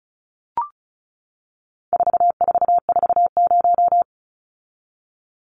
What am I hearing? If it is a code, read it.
Morse code, heard as a keyed tone, about 34 wpm, 700 Hz. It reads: 4440
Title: Numbers - 4-Digit 35wpm